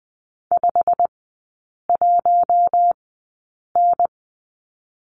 Morse code, heard as a keyed tone, about 20 wpm, spaced as for 8 wpm, 690 Hz. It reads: 51N
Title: 087 Lesson 15e - Number 5 - Three Characters 20wpm